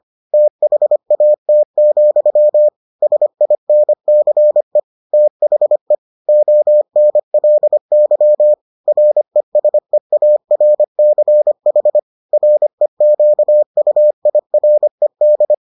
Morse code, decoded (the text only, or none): THAT, SINCE THE ONLY RESEARCH REQUIRED